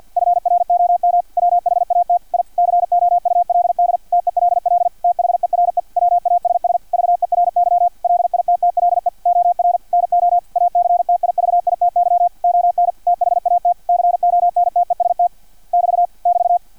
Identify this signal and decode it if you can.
Morse code, heard as a keyed tone, about 50 wpm, 700 Hz, with radio noise. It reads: FROM WHAT I COULD TELL THERE WASS VERY LITTLE OR NO ACTIVITY ON THAT CONTEST <BT> <BT>
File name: archived / CRSnet